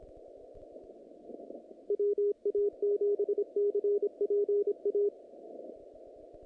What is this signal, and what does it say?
Morse code, heard as a keyed tone, about 25 wpm, 410 Hz. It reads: WA7CPA
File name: contests / naqp-cw-k5tr-op-k5pi- -2200